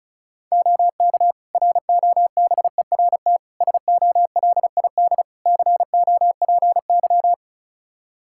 Morse code, heard as a keyed tone, about 34 wpm, 700 Hz. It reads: OK ROBERT SOLID COPY
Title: K1USN SST QSO Practice 35wpm